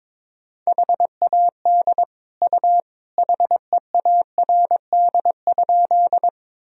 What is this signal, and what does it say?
Morse code, heard as a keyed tone, about 21 wpm, 700 Hz. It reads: HAD U HEARD?
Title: Sentences from Top 400 Words and Abbreviations 22wpm